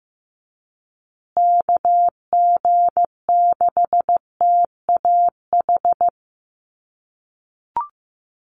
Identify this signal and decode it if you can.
Morse code, heard as a keyed tone, about 15 wpm, 700 Hz. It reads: KG6TAH
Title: USA Call Signs Repeated 3x 15wpm